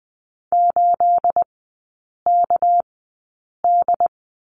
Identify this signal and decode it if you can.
Morse code, heard as a keyed tone, about 20 wpm, spaced as for 8 wpm, 700 Hz. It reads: 8KD
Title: 212 Lesson 34e - Number 8 - Three Characters 20wpm